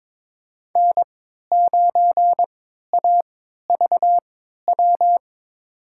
Morse code, heard as a keyed tone, about 22 wpm, spaced as for 13 wpm, 700 Hz. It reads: N9AVW
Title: USA Call Signs - 5 Characters - 3x Character Spacing - 22wpm